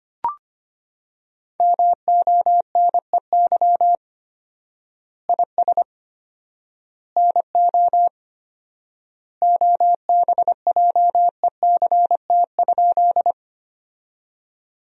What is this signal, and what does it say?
Morse code, heard as a keyed tone, about 25 wpm, 700 Hz. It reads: MONEY IS NO OBJECT?
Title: 4x Word-Spacing - Sentences from Top 500 Words - 25wpm